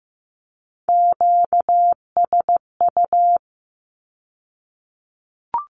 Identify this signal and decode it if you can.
Morse code, heard as a keyed tone, about 15 wpm, 700 Hz. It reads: QSU
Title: Q-Codes with Spoken Meaning - 15wpm